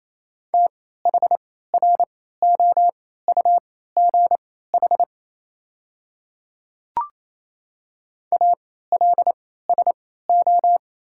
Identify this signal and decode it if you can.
Morse code, heard as a keyed tone, about 28 wpm, spaced as for 16 wpm, 700 Hz. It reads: THROUGH ALSO
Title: 3x Character Spacing - Top 500 Words - 28wpm